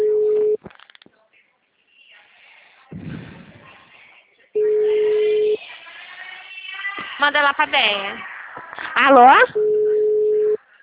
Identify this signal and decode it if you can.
Ringback tone (Europe, 425 Hz), with speech.